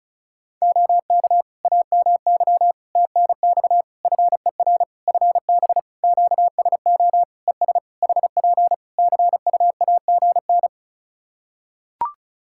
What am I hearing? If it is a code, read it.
Morse code, heard as a keyed tone, about 34 wpm, 700 Hz. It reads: OK AMY TNX FER FB QSO ES HP CUAGN
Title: Practice QSO - Exchange 4 - Ending - 35wpm